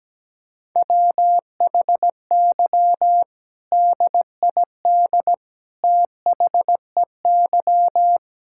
Morse code, heard as a keyed tone, about 17 wpm, 700 Hz. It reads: WHY DID THEY
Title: Sentences from Top 1000 Words - Review Part 6 - 17wpm